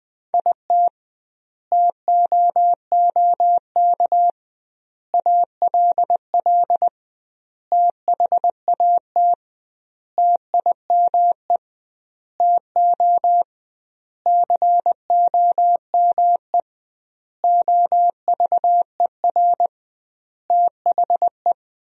Morse code, heard as a keyed tone, about 20 wpm, 700 Hz. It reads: IT TOOK ALL THAT TIME TO COME OVER THE
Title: 2x Word-Spacing - Sentences from Top 300 Words - 20wpm